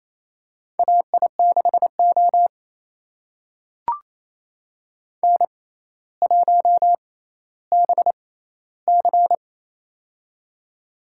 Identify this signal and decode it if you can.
Morse code, heard as a keyed tone, about 28 wpm, spaced as for 14 wpm, 700 Hz. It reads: AI6O N1BC
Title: USA Call Signs - 4 Characters - 6x Character Spacing - 28wpm